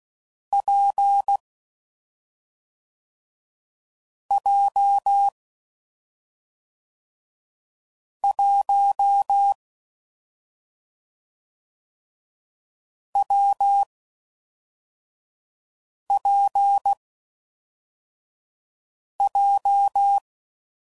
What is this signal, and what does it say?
Morse code, heard as a keyed tone, about 16 wpm, 800 Hz. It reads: P J 1 W P J
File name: Combinations / Difficult Combos / WPJ WPJ1